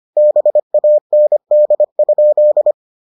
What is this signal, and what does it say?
Morse code, heard as a keyed tone, about 25 wpm, 600 Hz. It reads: BAND?